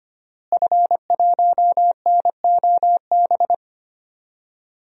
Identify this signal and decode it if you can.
Morse code, heard as a keyed tone, about 25 wpm, 700 Hz. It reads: F1NOB
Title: DXCC-100 Call Signs - 5 Characters - Repeated 3x - 25wpm